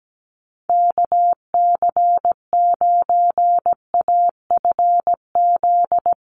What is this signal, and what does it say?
Morse code, heard as a keyed tone, about 17 wpm, 700 Hz. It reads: KC9AFZ
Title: USA Call Signs Repeated 3x 17wpm